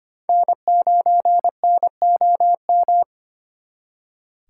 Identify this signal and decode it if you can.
Morse code, heard as a keyed tone, about 31 wpm, 700 Hz. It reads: N9NOM
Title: Speed Racing - USA 1x3 Call Signs 25wpm